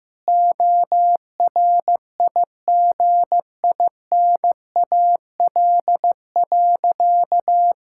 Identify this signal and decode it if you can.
Morse code, heard as a keyed tone, about 16 wpm, 700 Hz. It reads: ORIGINAL.
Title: Sentences from Top 1000 Words - Review Part 6 - 15wpm